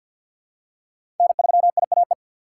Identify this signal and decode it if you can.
Morse code, heard as a keyed tone, about 47 wpm, 690 Hz. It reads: N3IAE
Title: USA Call Signs - 5 Characters - 6x Character Spacing - 50wpm